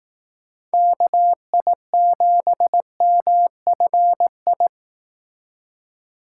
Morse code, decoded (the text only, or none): KI7MFI